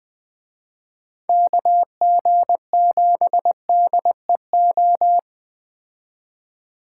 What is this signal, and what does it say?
Morse code, heard as a keyed tone, about 20 wpm, 700 Hz. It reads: KG7DEO